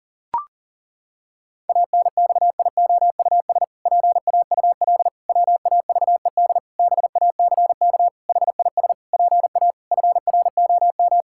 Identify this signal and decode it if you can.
Morse code, heard as a keyed tone, about 40 wpm, 700 Hz. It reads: ANXIOUS PAUL WAVED BACK HIS PA FROM
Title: Pangram Challenge v2 - 40wpm